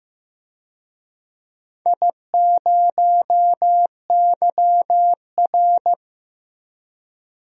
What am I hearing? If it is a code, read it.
Morse code, heard as a keyed tone, about 15 wpm, 700 Hz. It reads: I0YR
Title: DXCC-100 Call Signs - 4 Characters - Repeated 3x - 15wpm